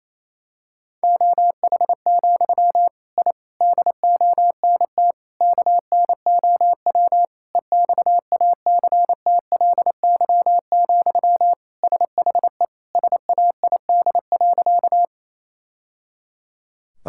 Morse code, heard as a keyed tone, about 28 wpm, 700 Hz. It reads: OH, I DONT KNOW EXACTLY, SHE SAID.
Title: Sentences from Top 1000 Words - Review Part 6 - 28wpm